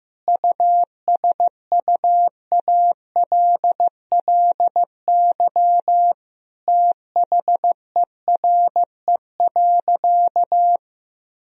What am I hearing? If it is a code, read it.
Morse code, heard as a keyed tone, about 16 wpm, 700 Hz. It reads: USUALLY THERE.
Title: Sentences from Top 1000 Words - Review Part 3 - 15wpm